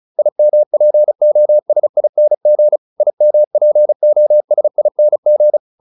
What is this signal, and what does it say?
Morse code, heard as a keyed tone, about 34 wpm, 600 Hz. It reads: IMPOSING IMPOSING